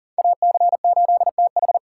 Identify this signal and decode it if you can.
Morse code, heard as a keyed tone, about 40 wpm, 700 Hz. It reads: AC8TH
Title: USA Call Signs - 4x Character Spacing - 40wpm